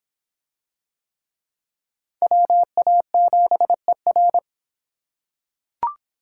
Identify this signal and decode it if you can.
Morse code, heard as a keyed tone, about 25 wpm, 700 Hz. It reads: WA7ER